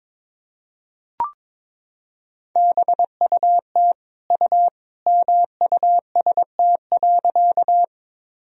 Morse code, heard as a keyed tone, about 22 wpm, 700 Hz. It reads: BUT U MUST.
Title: Sentences from Top 200 Words and Abbreviations 22wpm